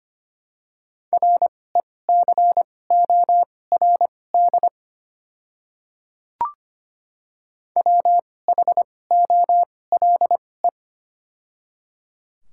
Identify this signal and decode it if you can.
Morse code, heard as a keyed tone, about 25 wpm, spaced as for 18 wpm, 700 Hz. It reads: RECORD WHOLE
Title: 2x Character Spacing - Top 500 Words - 25wpm